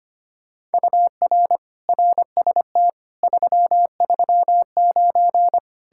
Morse code, heard as a keyed tone, about 33 wpm, 700 Hz. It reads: UR RST 339